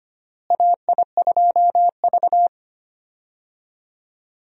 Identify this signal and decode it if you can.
Morse code, heard as a keyed tone, about 25 wpm, 700 Hz. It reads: AI2V
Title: USA Call Signs - 4 Characters - Repeated 3x - 25wpm